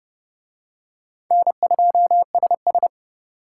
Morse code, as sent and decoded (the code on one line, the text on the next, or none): -. ..--- ... ...
N2SS